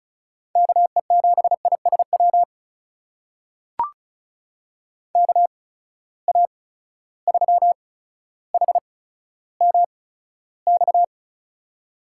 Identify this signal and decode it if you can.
Morse code, heard as a keyed tone, about 35 wpm, spaced as for 15 wpm, 700 Hz. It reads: KE7ISW KA3HMX
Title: USA Call Signs - 6 Characters - 8x Character Spacing - 35wpm